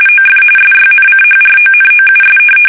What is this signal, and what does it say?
RTTY, about 45.45 baud. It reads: OF SYNCHRONOUS
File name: fsk850 / teletype-corp